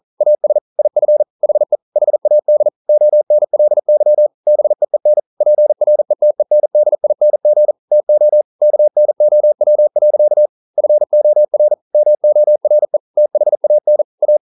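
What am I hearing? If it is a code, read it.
Morse code, heard as a keyed tone, about 42 wpm, 600 Hz. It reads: AS IF HE HAD ONLY BEEN PRETENDING TO KNOW. FOR MORE THAN A
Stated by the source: Morse code at 40 wpm, practice text, CW med Gnister.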